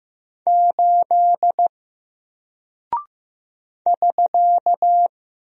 Morse code, heard as a keyed tone, about 15 wpm, 700 Hz. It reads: 8 <SK>
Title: Single Character (Letter:Number:Punctuation:Prosign) 15wpm